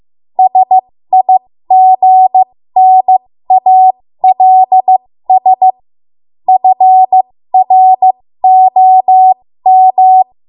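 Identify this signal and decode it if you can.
Morse code, heard as a keyed tone, about 15 wpm, spaced as for 13 wpm, 750 Hz. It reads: SIGNALS FROM
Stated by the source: 13 wpm Morse code practice archive transmission.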